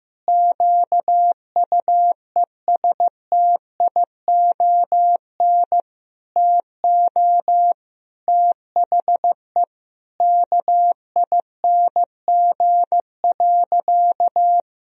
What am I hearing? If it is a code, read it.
Morse code, heard as a keyed tone, about 15 wpm, 700 Hz. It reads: QUESTION TO THE KING.
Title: Sentences from Top 1000 Words - Review Part 2 - 15wpm